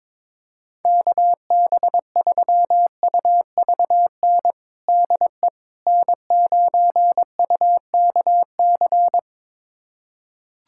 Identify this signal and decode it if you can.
Morse code, heard as a keyed tone, about 22 wpm, 700 Hz. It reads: KB3UVN DE N9UKC
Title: USA Callsign DE Callsign 22wpm